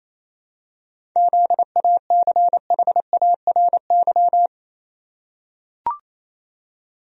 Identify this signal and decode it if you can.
Morse code, heard as a keyed tone, about 28 wpm, 700 Hz. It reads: ZACHARY
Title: USA Common Names - 28wpm